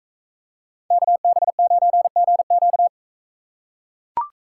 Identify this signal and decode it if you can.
Morse code, heard as a keyed tone, about 42 wpm, 700 Hz. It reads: KB9GQ